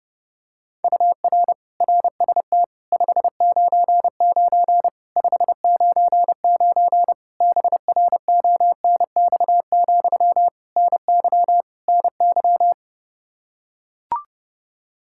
Morse code, heard as a keyed tone, about 30 wpm, 700 Hz. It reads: UR RST 599 599 BRONX, NY NY
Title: Practice QSO - Exchange 1 - Beginning - 30wpm